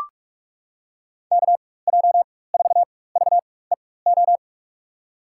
Morse code, heard as a keyed tone, about 43 wpm, spaced as for 21 wpm, 700 Hz. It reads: KJ4VEO
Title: USA Call Signs - 6 Characters - 4x Character Spacing - 45wpm